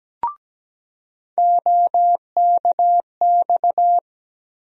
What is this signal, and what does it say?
Morse code, heard as a keyed tone, about 17 wpm, 700 Hz. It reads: OKX
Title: USA Call Sign Postfixes - 3 Letters 17wpm